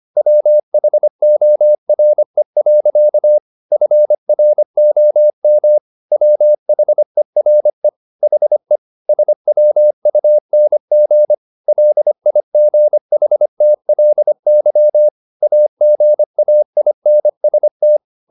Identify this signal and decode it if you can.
Morse code, heard as a keyed tone, about 25 wpm, 600 Hz. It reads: WHORE. FROM WHERE HE SWUNG LIGHTLY AGAINST
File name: the old man and the sea part 1 25wpm